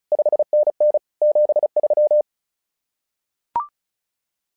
Morse code, heard as a keyed tone, about 36 wpm, 600 Hz. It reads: 5NN 73
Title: SOTA QSO Practice 35wpm